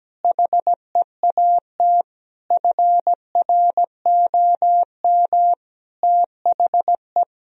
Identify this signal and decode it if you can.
Morse code, heard as a keyed tone, about 17 wpm, 700 Hz. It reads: HEAT FROM THE
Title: Sentences from Top 1000 Words - Review Part 3 - 17wpm